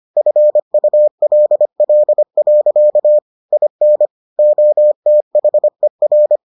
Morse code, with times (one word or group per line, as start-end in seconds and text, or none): 0.2-3.2 FULL.
3.5-4.1 IN
4.4-6.4 OTHER